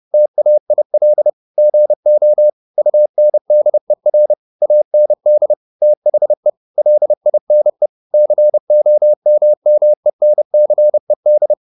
Morse code, heard as a keyed tone, about 30 wpm, 600 Hz. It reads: TAIL GO UNDER AND THE LINE COMMENCED
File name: the old man and the sea / the old man and the sea 1 30wpm